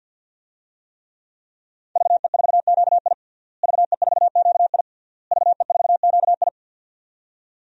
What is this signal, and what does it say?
Morse code, heard as a keyed tone, about 51 wpm, 700 Hz. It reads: VE4QI VE4QI VE4QI
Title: DXCC-100 Call Signs - 5 Characters - Repeated 3x - 50wpm